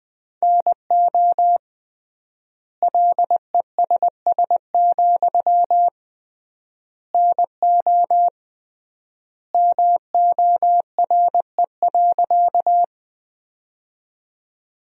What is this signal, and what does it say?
Morse code, heard as a keyed tone, about 21 wpm, 700 Hz. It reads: NO LESS, NO MORE.
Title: 3x Word-Spacing - Sentences from Top 400 Words - 20wpm